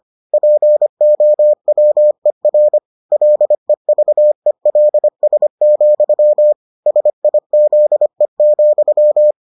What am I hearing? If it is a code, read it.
Morse code, heard as a keyed tone, about 25 wpm, 600 Hz. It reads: POWER LEVELS, SIZE,